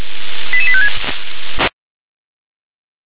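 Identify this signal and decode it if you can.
Five-tone selective call (ZVEI), heard as quick runs of five tones, tones of about 70 ms. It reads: 88856
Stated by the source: ZVEI five-tone selective call received on an SDR.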